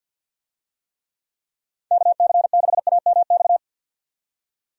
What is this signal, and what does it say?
Morse code, heard as a keyed tone, about 51 wpm, 700 Hz. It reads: KC6AMX
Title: USA Call Signs - 6 Characters - 50wpm